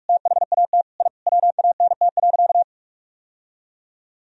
Morse code, heard as a keyed tone, about 43 wpm, 700 Hz. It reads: THAT I WANT.
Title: Sentences from Top 1000 Words - Review Part 3 - 45wpm